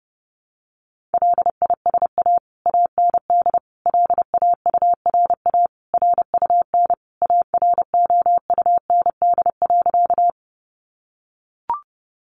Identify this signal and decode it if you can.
Morse code, heard as a keyed tone, about 30 wpm, 700 Hz. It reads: LISA AND LAURA RUN AROUND.
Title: Additional Easy Phrases 30wpm